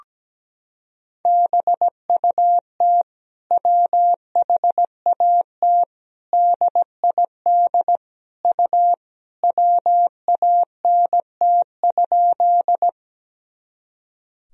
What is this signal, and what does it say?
Morse code, heard as a keyed tone, about 17 wpm, 700 Hz. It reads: BUT WHAT DID U WANT?
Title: Sentences from Top 200 Words and Abbreviations 17wpm